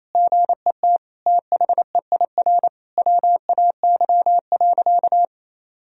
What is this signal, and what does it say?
Morse code, heard as a keyed tone, about 28 wpm, 700 Hz. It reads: GET THEIR WAY.
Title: Sentences from Top 1000 Words - Review Part 4 - 28wpm